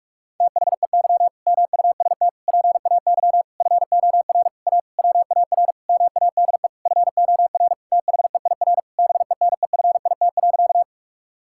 Morse code, heard as a keyed tone, about 46 wpm, 700 Hz. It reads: THEY MUST PAY FOR A WAR MADE FOR THEIR BENEFIT.
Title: Sentences from Top 1000 Words - Review Part 6 - 45wpm